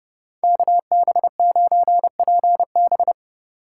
Morse code, heard as a keyed tone, about 33 wpm, 700 Hz. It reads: KB9PB